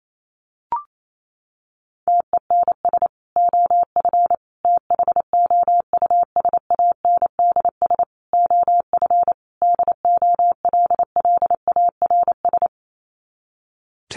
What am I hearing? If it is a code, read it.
Morse code, heard as a keyed tone, about 28 wpm, 700 Hz. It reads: TENS OF THOUSANDS OF DOLLARS
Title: Sets of 5 Words Encore2 28wpm